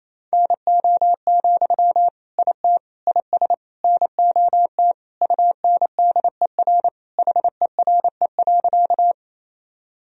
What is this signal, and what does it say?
Morse code, heard as a keyed tone, about 28 wpm, 700 Hz. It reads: NO, IT IS NOT UNDER HERE.
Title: Sentences from Top 1000 Words - Review Part 1 - 28wpm